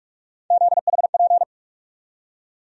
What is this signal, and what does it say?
Morse code, heard as a keyed tone, about 46 wpm, 700 Hz. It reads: ZHP
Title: USA Call Sign Postfixes - 3 Letters 45wpm